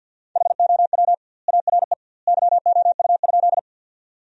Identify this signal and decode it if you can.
Morse code, heard as a keyed tone, about 50 wpm, 700 Hz. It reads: HOW ARE YOU?
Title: 2x Word-Spacing - Sentences from Top 100 Words - 50wpm